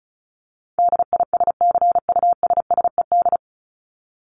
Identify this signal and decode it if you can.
Morse code, heard as a keyed tone, about 34 wpm, 700 Hz. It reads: DISCUSSED
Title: ED Suffix - Top 100 Words with Base Words 35wpm